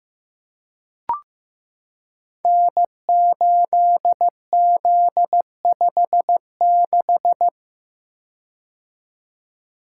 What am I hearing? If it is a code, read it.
Morse code, heard as a keyed tone, about 15 wpm, 700 Hz. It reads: N8Z56
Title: Code Groups - 5 Letter:Numbers - 15wpm